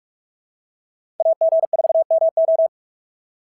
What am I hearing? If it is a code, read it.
Morse code, heard as a keyed tone, about 42 wpm, 650 Hz. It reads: AG4MO
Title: SOTA QSO Practice 45wpm